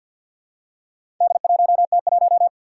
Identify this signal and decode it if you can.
Morse code, heard as a keyed tone, about 52 wpm, 700 Hz. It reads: D1T1
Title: Four Challenging Letter-Number Combinations 50wpm